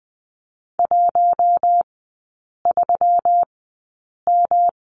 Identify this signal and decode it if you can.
Morse code, heard as a keyed tone, about 20 wpm, spaced as for 8 wpm, 690 Hz. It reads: 13M
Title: 198 Lesson 32e - Letter k - Three Characters 20wpm